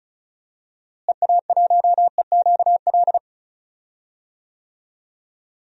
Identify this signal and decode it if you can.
Morse code, heard as a keyed tone, about 34 wpm, 700 Hz. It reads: EA1EQL
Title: DXCC-100 Call Signs - 6 Characters - Repeated 2x - 35wpm